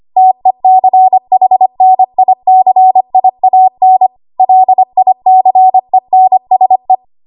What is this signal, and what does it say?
Morse code, heard as a keyed tone, about 25 wpm, 750 Hz. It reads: TECHNICIAN LICENSE
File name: Archive / 25 WPM / 25WPM